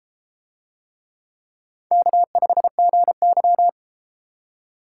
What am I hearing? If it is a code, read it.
Morse code, heard as a keyed tone, about 32 wpm, 700 Hz. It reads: K5GY